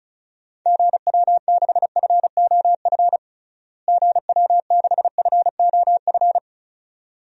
Morse code, heard as a keyed tone, about 34 wpm, 700 Hz. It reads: GW6FOF GW6FOF
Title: DXCC-100 Call Signs - 6 Characters - Repeated 2x - 35wpm